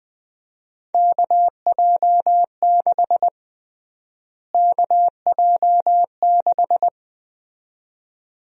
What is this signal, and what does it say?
Morse code, heard as a keyed tone, about 21 wpm, 700 Hz. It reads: KJ6 KJ6
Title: USA Call Sign Prefixes - Repeated 2x - 20wpm